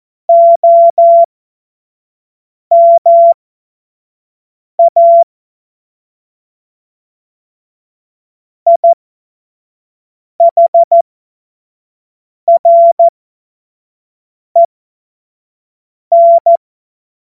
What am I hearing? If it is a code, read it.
Morse code, heard as a keyed tone, about 14 wpm, spaced as for 5 wpm, 680 Hz. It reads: OMA IHREN
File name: Morsekurs / Witze / Witz 26. Oktober 2021, 25 F70